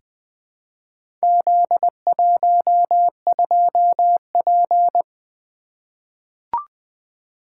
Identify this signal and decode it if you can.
Morse code, heard as a keyed tone, about 20 wpm, 700 Hz. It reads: Z12P
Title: Four Letter-Number Combinations - 20wpm